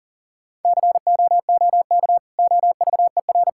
Morse code, heard as a keyed tone, about 40 wpm, 700 Hz. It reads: COOK OVER
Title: Sentences from Top 1000 Words - Review Part 6 - 40wpm